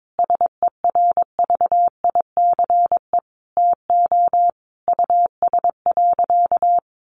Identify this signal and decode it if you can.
Morse code, heard as a keyed tone, about 22 wpm, 700 Hz. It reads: SERVICE TO US.
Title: Sentences from Top 1200 Words 22wpm